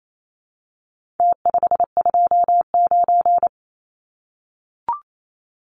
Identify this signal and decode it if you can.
Morse code, heard as a keyed tone, about 28 wpm, 700 Hz. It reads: T529
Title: Numbers - Contest Style - 4-Digit 28wpm